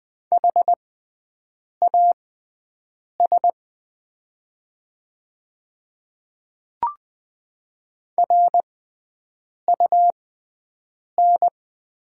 Morse code, heard as a keyed tone, about 20 wpm, spaced as for 7 wpm, 700 Hz. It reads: HAS RUN
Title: 6x Character Spacing - Top 300 Words - 20wpm